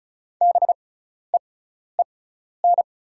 Morse code, heard as a keyed tone, about 36 wpm, spaced as for 12 wpm, 700 Hz. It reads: BEEN
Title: 6x Character Spacing - Top 100 Words - 35wpm